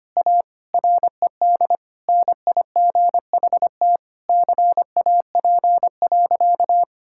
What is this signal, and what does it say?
Morse code, heard as a keyed tone, about 25 wpm, 700 Hz. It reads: A RED NIGHT CAP.
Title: Sentences from Top 1600 Words 25wpm